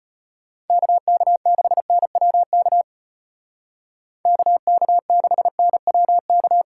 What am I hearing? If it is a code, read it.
Morse code, heard as a keyed tone, about 33 wpm, 700 Hz. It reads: KK6NWK KK6NWK